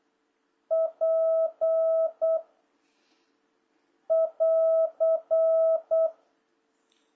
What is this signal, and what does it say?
Morse code, heard as a keyed tone, about 8 wpm, 660 Hz. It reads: P <AR>
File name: jh8cbh / 2026cw